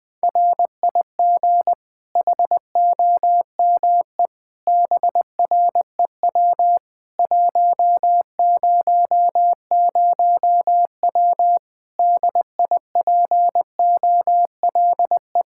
Morse code, heard as a keyed tone, about 20 wpm, 700 Hz. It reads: RIG HOME BREW 100W DIPOLE
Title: Practice QSO - Exchange 2 - Info - 20wpm